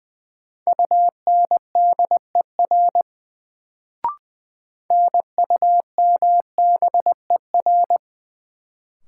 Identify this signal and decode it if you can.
Morse code, heard as a keyed tone, about 20 wpm, 700 Hz. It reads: UNDER NUMBER
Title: Sets of 2 Words - New Word Drill - 20wpm